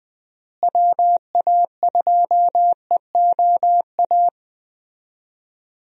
Morse code, heard as a keyed tone, about 20 wpm, 700 Hz. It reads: WA2EOA
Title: USA Call Signs - 6 Characters - Repeated 3x - 20wpm